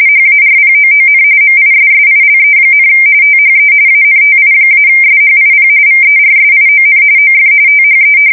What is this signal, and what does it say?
RTTY, about 45.45 baud. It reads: HARPOON MISSILE, THE ANTI-SHIP VERSION OF THE
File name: fsk170 / battleshipnj